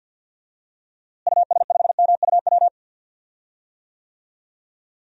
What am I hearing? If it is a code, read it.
Morse code, heard as a keyed tone, about 52 wpm, 700 Hz. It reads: US5MUW
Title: DXCC-100 Call Signs - 6 Characters - Repeated 3x - 50wpm